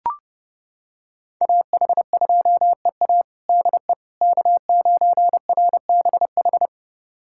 Morse code, heard as a keyed tone, about 30 wpm, 700 Hz. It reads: AH2EA DE K9RBH
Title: USA Callsign DE Callsign - 5 Characters - 30wpm